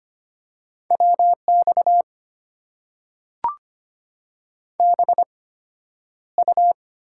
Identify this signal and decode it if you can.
Morse code, heard as a keyed tone, about 25 wpm, spaced as for 11 wpm, 700 Hz. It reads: WX BU
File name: Top 90 QSO Elements - 8x Character Spacing - 25wpm